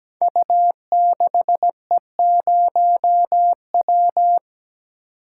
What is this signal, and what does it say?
Morse code, heard as a keyed tone, about 17 wpm, 700 Hz. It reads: U6E0W
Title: Code Groups - 5 Letter:Numbers - 17wpm